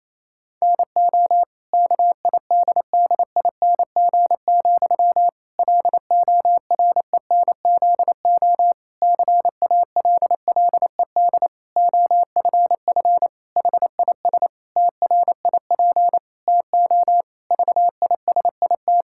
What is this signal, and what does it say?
Morse code, heard as a keyed tone, about 28 wpm, 700 Hz. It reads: NO KIDDING, LORENZO CALLED OFF HIS TRIP TO VISIT
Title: Pangram Challenge v2 - 28wpm